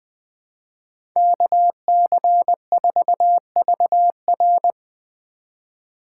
Morse code, heard as a keyed tone, about 20 wpm, 700 Hz. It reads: KC4VR
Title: USA Call Signs - 5 Characters - 2x Character Spacing - 20wpm